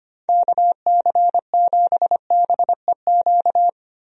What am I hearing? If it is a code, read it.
Morse code, heard as a keyed tone, about 25 wpm, 700 Hz. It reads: KC7BEQ
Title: USA Call Signs - 6 Characters - Repeated 2x - 25wpm